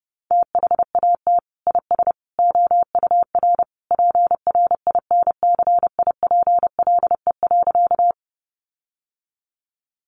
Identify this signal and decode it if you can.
Morse code, heard as a keyed tone, about 30 wpm, 700 Hz. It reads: THAT IS OUR PRINCIPLE.
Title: Sentences from Top 1400 Words 30wpm